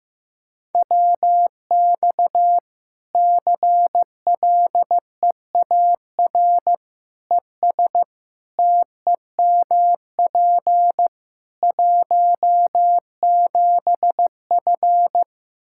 Morse code, heard as a keyed tone, about 16 wpm, 700 Hz. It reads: WX CLEAR ES TEMP 17F
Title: Practice QSO - Exchange 2 - Info - 15wpm